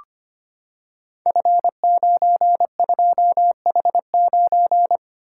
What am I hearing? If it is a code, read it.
Morse code, heard as a keyed tone, about 25 wpm, 700 Hz. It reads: F92H9
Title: Code Groups - 5 Letter:Numbers - 25wpm